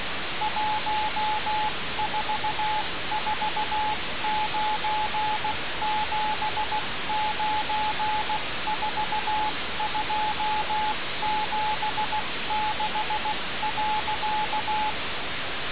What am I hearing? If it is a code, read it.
Morse code, heard as a keyed tone, about 16 wpm, 860 Hz, with radio noise. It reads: 1449794276.
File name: radiocap / REC z